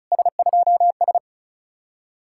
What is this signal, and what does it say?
Morse code, heard as a keyed tone, about 44 wpm, 700 Hz. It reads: S2S